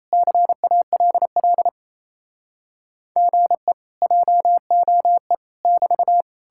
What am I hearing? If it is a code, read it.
Morse code, heard as a keyed tone, about 37 wpm, 700 Hz. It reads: CALL GE JOE <BT>